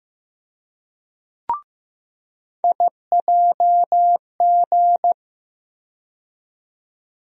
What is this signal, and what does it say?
Morse code, heard as a keyed tone, about 15 wpm, 700 Hz. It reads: IJG
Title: USA Call Sign Postfixes - 3 Letters 15wpm